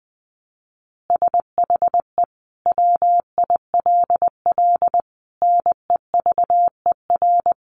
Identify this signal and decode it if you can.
Morse code, heard as a keyed tone, about 20 wpm, 700 Hz. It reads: SHE WILL NEVER
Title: Additional Easy Phrases 20wpm